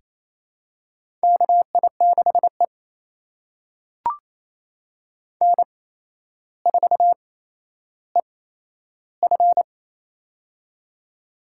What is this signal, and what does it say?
Morse code, heard as a keyed tone, about 28 wpm, spaced as for 12 wpm, 700 Hz. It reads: KI6E N4EF
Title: USA Call Signs - 4 Characters - 8x Character Spacing - 28wpm